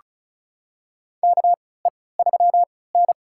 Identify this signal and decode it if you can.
Morse code, heard as a keyed tone, about 34 wpm, spaced as for 20 wpm, 700 Hz. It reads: KE3N
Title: USA Call Signs - 3x Character Spacing - 35wpm